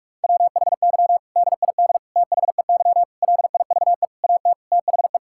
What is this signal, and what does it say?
Morse code, heard as a keyed tone, about 46 wpm, 700 Hz. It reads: WHY DID THEY LIVE AT THE
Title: Sentences from Top 1000 Words - Review Part 6 - 45wpm